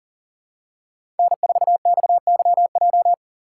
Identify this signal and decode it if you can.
Morse code, heard as a keyed tone, about 40 wpm, 700 Hz. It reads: N4XYJ
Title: USA Call Signs - 4x Character Spacing - 40wpm